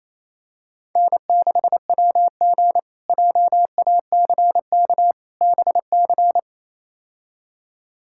Morse code, heard as a keyed tone, about 28 wpm, 700 Hz. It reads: N6WG JACK BC
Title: K1USN SST QSO Practice - 28wpm